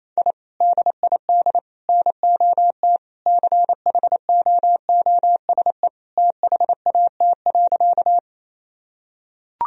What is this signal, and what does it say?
Morse code, heard as a keyed tone, about 28 wpm, 700 Hz. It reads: I DID NOT CHOOSE THAT.
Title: Sentences from Top 1000 Words - Review Part 4 - 28wpm